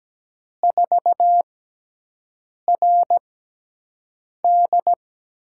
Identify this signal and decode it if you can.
Morse code, heard as a keyed tone, about 17 wpm, spaced as for 6 wpm, 700 Hz. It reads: 4RD
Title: USA Call Signs - 6x Character Spacing - 17wpm